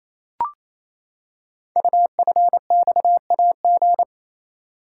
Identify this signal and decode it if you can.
Morse code, heard as a keyed tone, about 27 wpm, 700 Hz. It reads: UFXAG
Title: Code Groups - 5 Letters - 28wpm